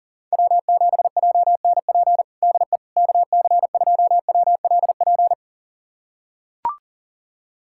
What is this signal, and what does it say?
Morse code, heard as a keyed tone, about 40 wpm, 700 Hz. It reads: W7JNP DE KC2WLP
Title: USA Callsign DE Callsign 40wpm